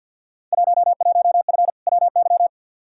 Morse code, heard as a keyed tone, about 48 wpm, 700 Hz. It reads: 11F WY
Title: Field Day Exchanges Encore - 50wpm